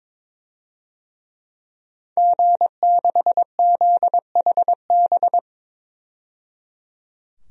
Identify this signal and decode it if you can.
Morse code, heard as a keyed tone, about 22 wpm, 700 Hz. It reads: G6ZHB
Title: DXCC-100 Call Signs - 5 Characters - 22wpm